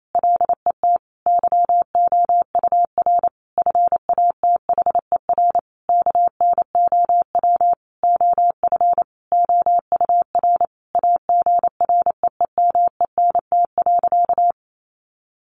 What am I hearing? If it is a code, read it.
Morse code, heard as a keyed tone, about 28 wpm, 700 Hz. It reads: LET YOUR FATHER KNOW OF OUR AGREEMENT.
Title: Sentences from Top 1500 Words 28wpm